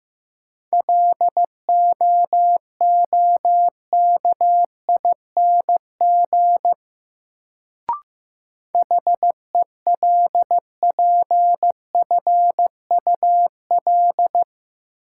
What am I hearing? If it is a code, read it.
Morse code, heard as a keyed tone, about 15 wpm, 700 Hz. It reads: LOOKING HELPFUL
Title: Sentences from Top 1000 Words - Review Part 6 - New Word Drill - 15wpm